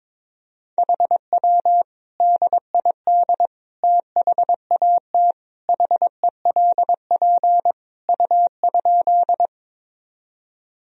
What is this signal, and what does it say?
Morse code, heard as a keyed tone, about 22 wpm, 700 Hz. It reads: HW DID THAT HELP U?
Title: Sentences from Top 200 Words and Abbreviations 22wpm